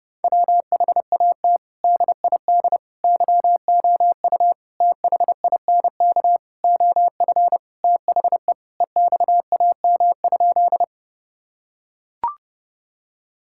Morse code, heard as a keyed tone, about 30 wpm, 700 Hz. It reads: WHAT DID YOU THINK OF THE EXAM?
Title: Sentences from Top 1000 Words - Review Part 5 - 30wpm